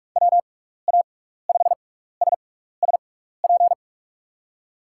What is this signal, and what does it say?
Morse code, heard as a keyed tone, about 44 wpm, spaced as for 15 wpm, 700 Hz. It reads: WA5SSP
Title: USA Call Signs - 6 Characters - 6x Character Spacing - 45wpm